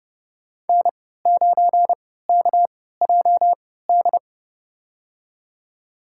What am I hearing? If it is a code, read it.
Morse code, heard as a keyed tone, about 30 wpm, spaced as for 17 wpm, 700 Hz. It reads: N9KJD
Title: USA Call Signs - 5 Characters - 3x Character Spacing - 30wpm